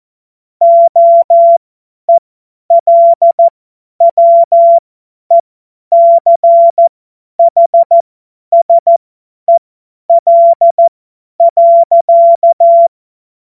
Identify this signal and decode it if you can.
Morse code, heard as a keyed tone, about 14 wpm, spaced as for 10 wpm, 680 Hz. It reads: OELWECHSEL.
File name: Morsekurs / Witze / Witz 13. Januar 2026 50 F70